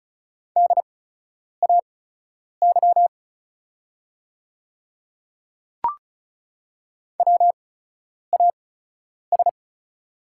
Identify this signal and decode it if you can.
Morse code, heard as a keyed tone, about 36 wpm, spaced as for 10 wpm, 700 Hz. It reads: DAY WAS
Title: 8x Character Spacing - Top 100 Words - 35wpm